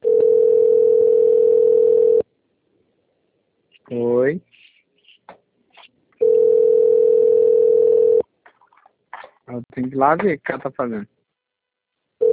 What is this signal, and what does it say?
Ringback tone (North America), 440 + 480 Hz, with speech.
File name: rg-1600-unknown- - - .48704